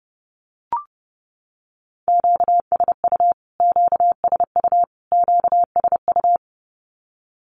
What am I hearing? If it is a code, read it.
Morse code, heard as a keyed tone, about 30 wpm, 700 Hz. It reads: QSU QSU QSU
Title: Q-Codes Repeated 3x 30wpm